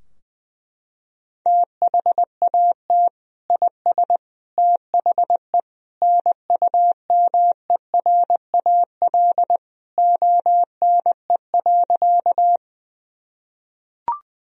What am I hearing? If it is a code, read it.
Morse code, heard as a keyed tone, about 20 wpm, 700 Hz. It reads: THAT IS THE NUMERAL ONE.
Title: Sentences from Top 1000 Words - Review Part 2 - 20wpm